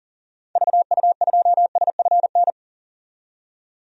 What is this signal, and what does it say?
Morse code, heard as a keyed tone, about 38 wpm, 700 Hz. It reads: VU2SFN
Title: DXCC-100 Call Signs - 6 Characters - 40wpm